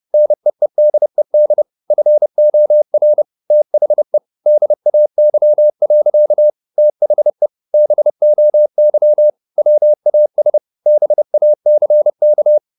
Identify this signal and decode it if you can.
Morse code, heard as a keyed tone, about 30 wpm, 600 Hz. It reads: NEEDED FOR THE DAY. THE BOY WAS BACK
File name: the old man and the sea / the old man and the sea 1 30wpm